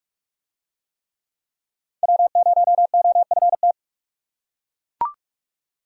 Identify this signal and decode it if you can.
Morse code, heard as a keyed tone, about 42 wpm, 700 Hz. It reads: W0OFT